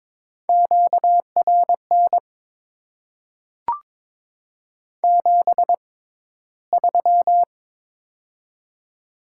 Morse code, heard as a keyed tone, about 22 wpm, spaced as for 13 wpm, 700 Hz. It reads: QRN 73
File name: Top 90 QSO Elements - 6x Character Spacing - 22wpm